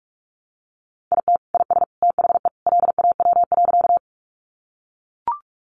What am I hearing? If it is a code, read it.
Morse code, heard as a keyed tone, about 44 wpm, 700 Hz. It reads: IT IS THE LAW.
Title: Sentences from Top 600 Words 45wpm